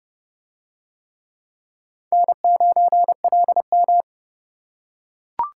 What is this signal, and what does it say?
Morse code, heard as a keyed tone, about 28 wpm, 700 Hz. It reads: N9LM